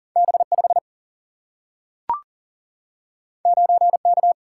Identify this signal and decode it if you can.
Morse code, heard as a keyed tone, about 40 wpm, 700 Hz. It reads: B5 9K
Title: Single-Letter Single-Number 40wpm